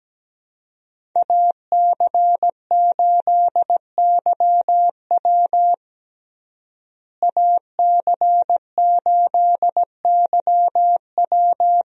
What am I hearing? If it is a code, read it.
Morse code, heard as a keyed tone, about 17 wpm, 700 Hz. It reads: AC8YW AC8YW
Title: USA Call Signs - 5 Characters - Repeated 2x - 17wpm